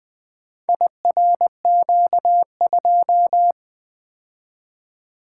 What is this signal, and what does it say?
Morse code, heard as a keyed tone, about 20 wpm, 700 Hz. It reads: IRQ2
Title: Four Letter-Number Combinations - 20wpm